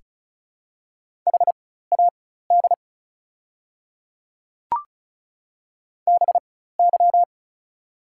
Morse code, heard as a keyed tone, about 36 wpm, spaced as for 16 wpm, 690 Hz. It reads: HAD BY
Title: 4x Character Spacing - Top 100 Words - 35wpm